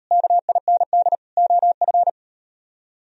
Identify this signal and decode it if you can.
Morse code, heard as a keyed tone, about 36 wpm, 700 Hz. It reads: KIND OF